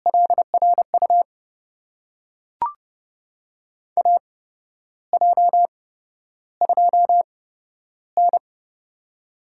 Morse code, heard as a keyed tone, about 30 wpm, spaced as for 12 wpm, 700 Hz. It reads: LRU AJ2N
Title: USA Call Signs - 8x Character Spacing - 30wpm